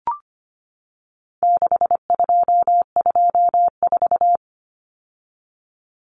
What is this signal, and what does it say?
Morse code, heard as a keyed tone, about 25 wpm, 700 Hz. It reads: 6224
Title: Common Cut Numbers - 4-Digit 25wpm